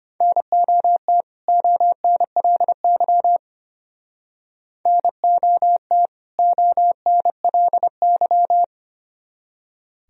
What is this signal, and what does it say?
Morse code, heard as a keyed tone, about 29 wpm, 700 Hz. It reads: NOT ONLY NOT ONLY
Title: Speed-Racing - Sets of 2 Words 25wpm